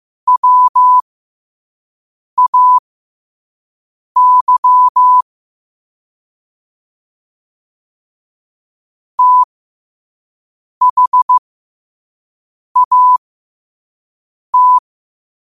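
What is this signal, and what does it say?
Morse code, heard as a keyed tone, about 15 wpm, spaced as for 5 wpm, 1000 Hz. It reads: WAY THAT
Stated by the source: Quote-of-the-day Morse code practice at 5 wpm.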